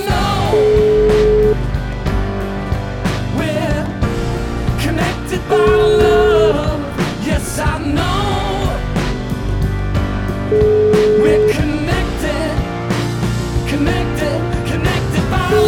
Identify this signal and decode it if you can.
Ringback tone (Europe, 425 Hz), with noise.